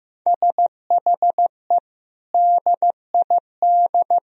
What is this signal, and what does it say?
Morse code, heard as a keyed tone, about 15 wpm, 700 Hz. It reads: SHE DID
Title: Sentences from Top 1000 Words - Review Part 4 - 15wpm